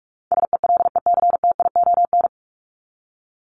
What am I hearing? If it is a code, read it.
Morse code, heard as a keyed tone, about 44 wpm, 700 Hz. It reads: SELECTION
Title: Top 1251-1500 Words 45wpm